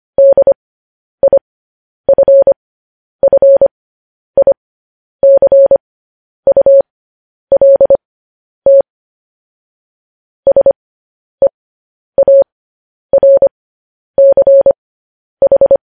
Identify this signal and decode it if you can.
Morse code, heard as a keyed tone, about 25 wpm, spaced as for 10 wpm, 550 Hz. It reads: DIFFICULT SEARCH
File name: podcast / News headlines-1x-25x10